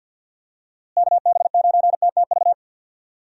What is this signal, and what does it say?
Morse code, heard as a keyed tone, about 50 wpm, 700 Hz. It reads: KB9TTV
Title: USA Call Signs - 6 Characters - Repeated 3x - 50wpm